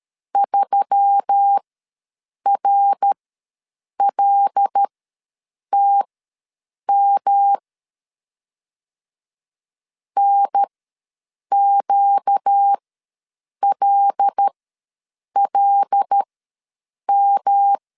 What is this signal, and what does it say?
Morse code, heard as a keyed tone, about 13 wpm, spaced as for 7 wpm, 800 Hz. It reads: 3RLTM NQLLM
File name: LEZIONI / 5 LBVC / 5 L ws 12wpm 2958